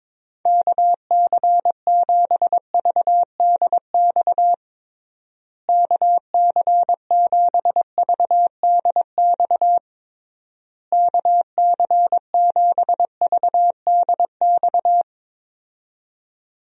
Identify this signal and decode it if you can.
Morse code, heard as a keyed tone, about 21 wpm, 700 Hz. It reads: KC7VDX KC7VDX KC7VDX
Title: USA Call Signs - 6 Characters - Repeated 3x - 22wpm